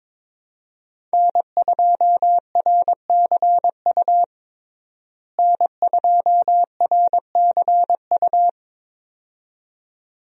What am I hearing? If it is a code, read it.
Morse code, heard as a keyed tone, about 22 wpm, 700 Hz. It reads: N2RCU N2RCU
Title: USA Call Signs - 5 Characters - Repeated 2x - 22wpm